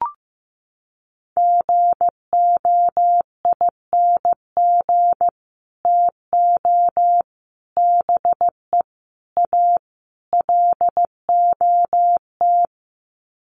Morse code, heard as a keyed tone, about 15 wpm, 700 Hz. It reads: GOING TO BE A LOT
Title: Sets of 5 Words Encore 15wpm